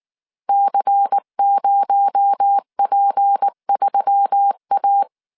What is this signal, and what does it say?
Morse code, heard as a keyed tone, about 19 wpm, 800 Hz. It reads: C0P3A
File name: CW / DET / 1 6 1500C S C 18wpm 2356